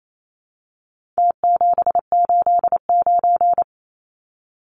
Morse code, heard as a keyed tone, about 28 wpm, 700 Hz. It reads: T789
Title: Numbers - Contest Style - 4-Digit 28wpm